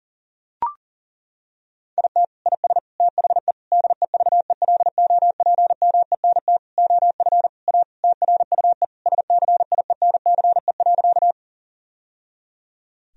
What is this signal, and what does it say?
Morse code, heard as a keyed tone, about 40 wpm, 700 Hz. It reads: IT IS THE DEVELOPMENT OF A TRUE SCIENCE.
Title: Sentences from Top 1000 Words - Review Part 4 - 40wpm